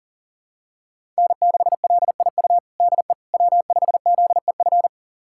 Morse code, heard as a keyed tone, about 40 wpm, 700 Hz. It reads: N6LIU DE W5ZEF
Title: USA Callsign DE Callsign - 5 Characters - 40wpm